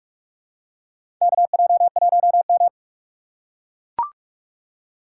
Morse code, heard as a keyed tone, about 43 wpm, 700 Hz. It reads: KJ1M
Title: USA Call Signs - 4 Characters - 4x Character Spacing - 45wpm